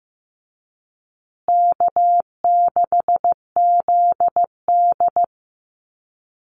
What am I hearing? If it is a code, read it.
Morse code, heard as a keyed tone, about 15 wpm, 700 Hz. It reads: K6ZD
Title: Easy Call Signs Repeated 3x - 15wpm